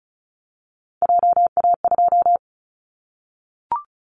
Morse code, heard as a keyed tone, about 34 wpm, 700 Hz. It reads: JA2
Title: Callsign Prefixes for Top 25 DXCC - Repeated 3x - 35wpm